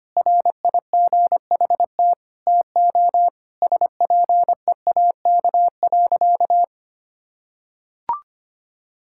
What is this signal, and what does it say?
Morse code, heard as a keyed tone, about 25 wpm, 700 Hz. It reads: RIGHT TO SPEAK.
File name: practice / 25wpm / Sentences with 3-Word Phrases - 25wpm